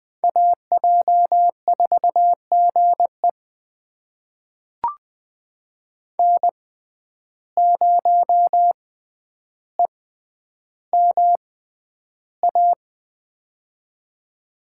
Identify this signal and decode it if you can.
Morse code, heard as a keyed tone, about 20 wpm, spaced as for 10 wpm, 700 Hz. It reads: AJ4GE N0EMA
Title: USA Call Signs - 5 Characters - 6x Character Spacing - 20wpm